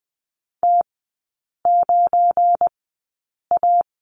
Morse code, heard as a keyed tone, about 20 wpm, spaced as for 8 wpm, 700 Hz. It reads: T9A
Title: 218 Lesson 35e - Number 0 - Three Characters 20wpm